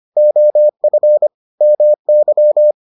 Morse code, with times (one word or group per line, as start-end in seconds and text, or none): 0.2-1.3 OF
1.6-2.7 MY